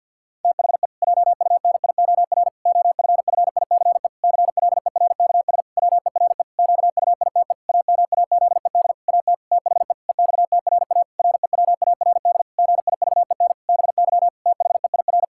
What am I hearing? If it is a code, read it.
Morse code, heard as a keyed tone, about 50 wpm, 700 Hz. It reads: THE JUNIOR OFFICE CLERKS WERE QUITE AMAZED AT THE EXTRA REWARD GIVEN BY THEIR
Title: Pangram Challenge v2 - 50wpm